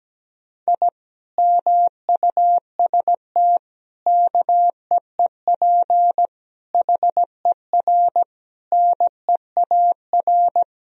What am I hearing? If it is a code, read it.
Morse code, heard as a keyed tone, about 17 wpm, 700 Hz. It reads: I MUST KEEP HER NEAR
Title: Sentences from Top 1000 Words - Review Part 1 - 17wpm